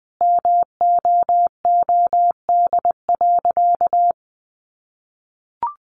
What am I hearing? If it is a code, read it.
Morse code, heard as a keyed tone, about 20 wpm, 700 Hz. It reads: MOOD.
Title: Sentences from Top 1100 Words 20wpm